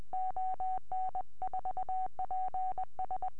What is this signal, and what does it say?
Morse code, heard as a keyed tone, about 20 wpm, 750 Hz. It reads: ON4PS